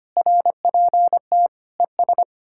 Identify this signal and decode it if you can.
Morse code, heard as a keyed tone, about 33 wpm, 700 Hz. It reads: RPT ES